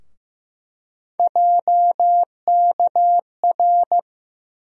Morse code, heard as a keyed tone, about 15 wpm, 700 Hz. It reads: JKR
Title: USA Call Sign Postfixes - 3 Letters 15wpm